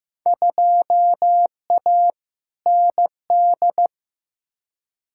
Morse code, heard as a keyed tone, about 15 wpm, 700 Hz. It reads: 2A ND
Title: Field Day Exchanges Encore - 15wpm